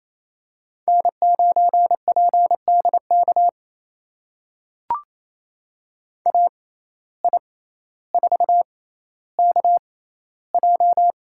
Morse code, heard as a keyed tone, about 28 wpm, spaced as for 14 wpm, 700 Hz. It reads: N9PDK AI4KJ
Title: USA Call Signs - 5 Characters - 6x Character Spacing - 28wpm